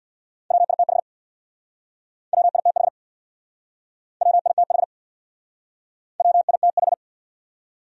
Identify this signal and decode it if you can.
Morse code, heard as a keyed tone, about 64 wpm, 700 Hz. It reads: WITH WITH WITH WITH